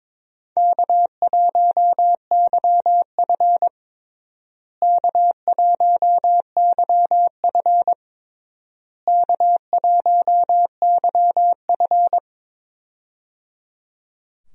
Morse code, heard as a keyed tone, about 22 wpm, 700 Hz. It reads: K1YF K1YF K1YF
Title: USA Call Signs - 4 Characters - Repeated 3x - 22wpm